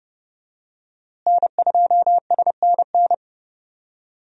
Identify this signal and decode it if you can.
Morse code, heard as a keyed tone, about 32 wpm, 700 Hz. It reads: N2SNN
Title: USA Call Signs - 5 Characters - Repeated 3x - 30wpm